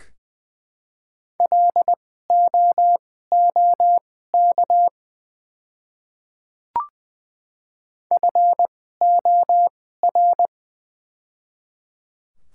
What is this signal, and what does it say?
Morse code, heard as a keyed tone, about 20 wpm, spaced as for 15 wpm, 700 Hz. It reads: LOOK FOR
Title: 2x Character Spacing - Top 100 Words - 20wpm